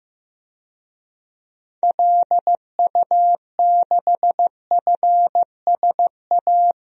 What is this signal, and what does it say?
Morse code, heard as a keyed tone, about 15 wpm, 700 Hz. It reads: LU6FSA
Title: DXCC-100 Call Signs - 6 Characters - Repeated 3x - 15wpm